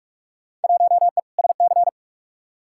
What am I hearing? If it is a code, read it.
Morse code, heard as a keyed tone, about 47 wpm, 700 Hz. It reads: 1E SC
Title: Field Day Exchanges - 45wpm